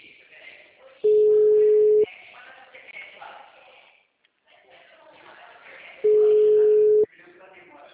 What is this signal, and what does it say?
Ringback tone (Europe, 425 Hz), with speech.